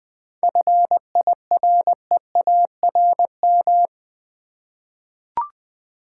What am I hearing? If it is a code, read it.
Morse code, heard as a keyed tone, about 20 wpm, 700 Hz. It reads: FIREARM
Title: 7 Letter Words - First 3 Letters Given - 20wpm